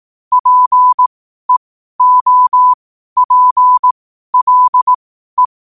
Morse code, heard as a keyed tone, about 18 wpm, spaced as for 12 wpm, 1000 Hz. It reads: PEOPLE
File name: qotdfiles / 12wpm